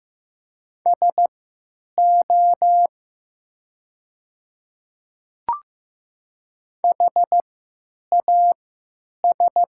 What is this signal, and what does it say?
Morse code, heard as a keyed tone, about 15 wpm, spaced as for 9 wpm, 700 Hz. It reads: SO HAS
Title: 3x Character Spacing - Top 100 Words - 15wpm